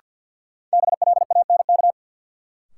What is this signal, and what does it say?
Morse code, heard as a keyed tone, about 52 wpm, 700 Hz. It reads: BLANK
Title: Sentences from Top 1500 Words - New Word Drill - 50wpm